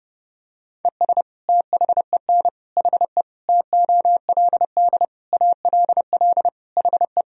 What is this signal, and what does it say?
Morse code, heard as a keyed tone, about 32 wpm, 700 Hz. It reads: ES THEN HE TOLD ALL HE
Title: Sentences from Top 400 Words and Abbreviations 30wpm